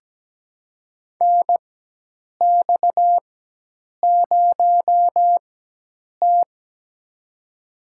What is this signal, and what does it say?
Morse code, heard as a keyed tone, about 17 wpm, spaced as for 8 wpm, 700 Hz. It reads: NX0T
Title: USA Call Signs - 4 Characters - 4x Character Spacing - 17wpm